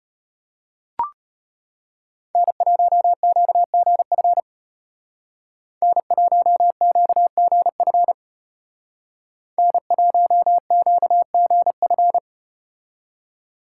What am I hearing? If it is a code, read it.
Morse code, heard as a keyed tone, about 33 wpm, 700 Hz. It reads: N1QGF N1QGF N1QGF